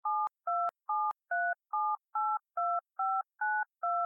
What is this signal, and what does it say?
DTMF, heard as telephone touch tones, tones of about 250 ms. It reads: 7273782592